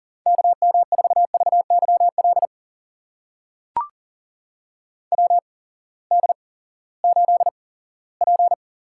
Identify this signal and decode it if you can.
Morse code, heard as a keyed tone, about 40 wpm, spaced as for 20 wpm, 700 Hz. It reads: KM4VYL WD8P
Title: USA Call Signs - 6 Characters - 8x Character Spacing - 40wpm